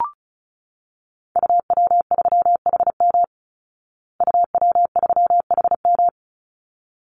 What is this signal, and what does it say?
Morse code, heard as a keyed tone, about 34 wpm, 700 Hz. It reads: UW3HM UW3HM
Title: Call Signs Repeated 3x - 35wpm